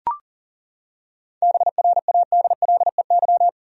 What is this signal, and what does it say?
Morse code, heard as a keyed tone, about 40 wpm, 700 Hz. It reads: BRADLEY
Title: USA Common Names - 40wpm